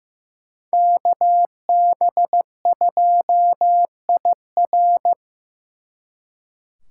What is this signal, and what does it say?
Morse code, heard as a keyed tone, about 18 wpm, 700 Hz. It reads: KB2IR
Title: Speed Racing - USA Call Signs 15wpm